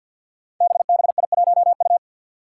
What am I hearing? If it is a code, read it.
Morse code, heard as a keyed tone, about 46 wpm, 700 Hz. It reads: BBI1U
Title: Five Challenging Letter-Number Combinations 50wpm